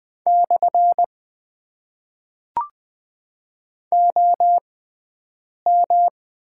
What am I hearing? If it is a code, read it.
Morse code, heard as a keyed tone, about 20 wpm, spaced as for 7 wpm, 700 Hz. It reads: / OM
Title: Top 90 QSO Elements - 6x Character Spacing - 20wpm